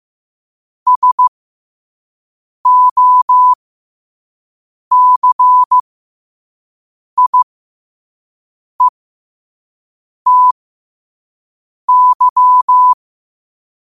Morse code, heard as a keyed tone, about 15 wpm, spaced as for 5 wpm, 1000 Hz. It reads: SOCIETY
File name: qotdfiles / 5wpm